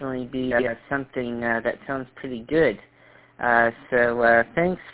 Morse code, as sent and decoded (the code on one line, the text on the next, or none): none